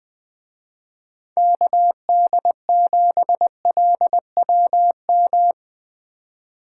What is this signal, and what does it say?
Morse code, heard as a keyed tone, about 21 wpm, 700 Hz. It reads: KD7LWM
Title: Speed Racing - USA 2x3 Call Signs 20wpm